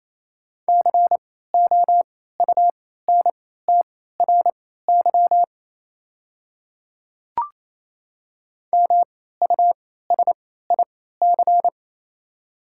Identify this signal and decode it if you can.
Morse code, heard as a keyed tone, about 28 wpm, spaced as for 16 wpm, 700 Hz. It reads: COUNTRY MUSIC
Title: 3x Character Spacing - Top 500 Words - 28wpm